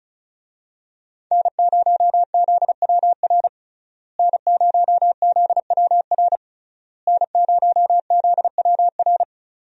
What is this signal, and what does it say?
Morse code, heard as a keyed tone, about 36 wpm, 700 Hz. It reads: N0ZWR N0ZWR N0ZWR
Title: USA Call Signs - 5 Characters - Repeated 3x - 35wpm